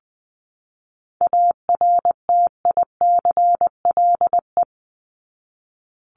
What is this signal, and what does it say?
Morse code, heard as a keyed tone, about 20 wpm, 700 Hz. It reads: ARTICLE
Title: Top 751-1000 Words 20wpm